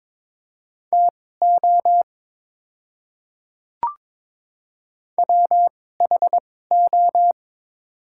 Morse code, heard as a keyed tone, about 21 wpm, spaced as for 16 wpm, 700 Hz. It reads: TO WHO
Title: 2x Character Spacing - Top 100 Words - 22wpm